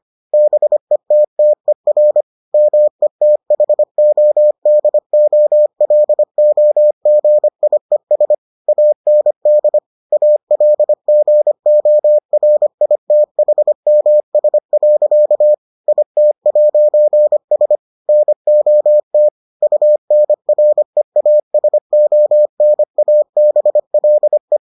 Morse code, heard as a keyed tone, about 25 wpm, 600 Hz. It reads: BETTER METHODOLOGIES AND ALGORITHMS. IT'S NOT UNREASONABLE